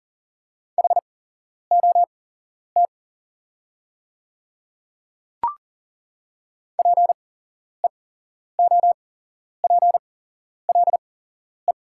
Morse code, heard as a keyed tone, about 40 wpm, spaced as for 11 wpm, 690 Hz. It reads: HOT PEOPLE
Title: 8x Character Spacing - Top 100 Words - 40wpm